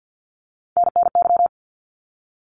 Morse code, heard as a keyed tone, about 50 wpm, 700 Hz. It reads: NNY
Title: ARRL Sections 50wpm